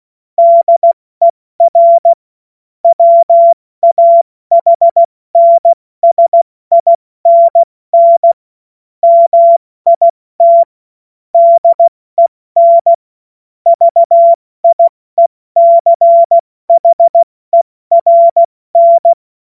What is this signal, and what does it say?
Morse code, heard as a keyed tone, about 16 wpm, 680 Hz. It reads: DER WAHNSINN MIT DEN VIECHERN